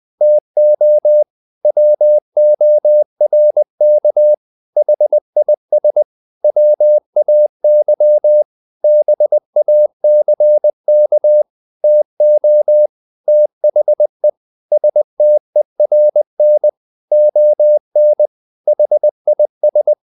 Morse code, heard as a keyed tone, about 20 wpm, 600 Hz. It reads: TO WORK HIS WAY BACK TO THE STERN ON HIS
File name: the old man and the sea part 2 20wpm